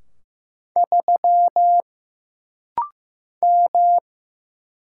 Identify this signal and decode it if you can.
Morse code, heard as a keyed tone, about 15 wpm, 700 Hz. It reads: 3 M
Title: Single Letter-Number Alt - Rapid-Fire 15wpm